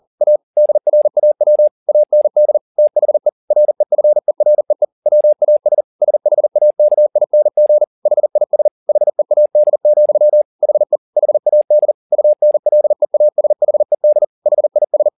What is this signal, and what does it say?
Morse code, heard as a keyed tone, about 40 wpm, 600 Hz. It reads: A DRAW AND THE REFEREE WAS SHAKING HIS HEAD, HE HAD UNLEASHED HIS
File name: the old man and the sea 2 40wpm